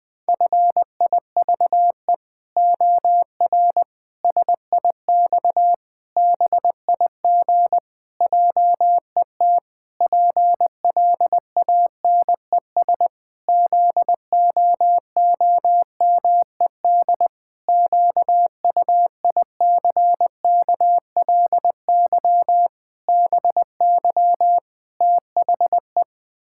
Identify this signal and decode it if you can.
Morse code, heard as a keyed tone, about 20 wpm, 700 Hz. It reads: FIVE OR SIX BIG JET PLANES ZOOMED QUICKLY BY THE
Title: Pangram Challenge v2 - 20wpm